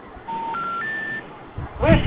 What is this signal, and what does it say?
Special information tone, with speech.